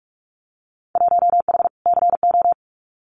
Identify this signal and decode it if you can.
Morse code, heard as a keyed tone, about 44 wpm, 700 Hz. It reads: 1H CO
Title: Winter Field Day Exchanges 45wpm